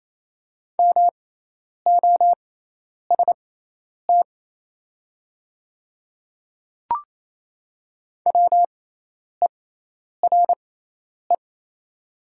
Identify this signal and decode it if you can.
Morse code, heard as a keyed tone, about 27 wpm, spaced as for 10 wpm, 700 Hz. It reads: MOST WERE
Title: 6x Character Spacing - Top 100 Words - 28wpm